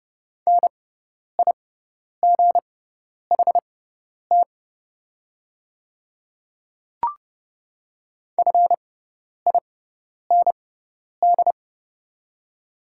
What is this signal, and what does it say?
Morse code, heard as a keyed tone, about 30 wpm, spaced as for 10 wpm, 690 Hz. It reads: NIGHT FIND
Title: 6x Character Spacing - Top 500 Words - 30wpm